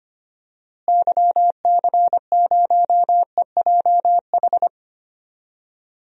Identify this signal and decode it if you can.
Morse code, heard as a keyed tone, about 25 wpm, 700 Hz. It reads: YC0EJH
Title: DXCC-100 Call Signs - 6 Characters - Repeated 3x - 25wpm